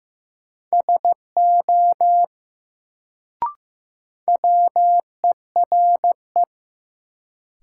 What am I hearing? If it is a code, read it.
Morse code, heard as a keyed tone, about 15 wpm, 700 Hz. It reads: SO WERE
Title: Sets of 2 Words - New Word Drill - 15wpm